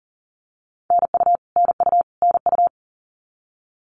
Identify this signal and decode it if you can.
Morse code, heard as a keyed tone, about 40 wpm, 700 Hz. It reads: NU NU NU
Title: Canadian Provinces and Territories - Repeated 3x - 40wpm